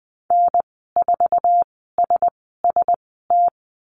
Morse code, heard as a keyed tone, about 20 wpm, spaced as for 15 wpm, 700 Hz. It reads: N4SST
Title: 048 Lesson 9g - Number 4 - Call Signs 20wpm